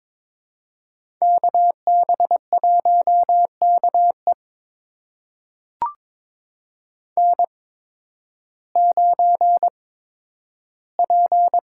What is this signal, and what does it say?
Morse code, heard as a keyed tone, about 23 wpm, spaced as for 12 wpm, 700 Hz. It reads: KB1KE N9P
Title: USA Call Signs - 5 Characters - 8x Character Spacing - 22wpm